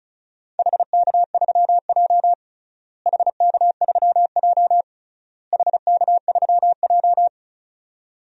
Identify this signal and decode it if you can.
Morse code, heard as a keyed tone, about 36 wpm, 700 Hz. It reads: HK3J HK3J HK3J
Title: DXCC-100 Call Signs - 4 Characters - Repeated 3x - 35wpm